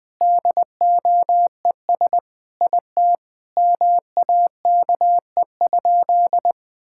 Morse code, heard as a keyed tone, about 20 wpm, 700 Hz. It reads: DOES IT MAKE?
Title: Sentences with 8-Word Phrases - 20wpm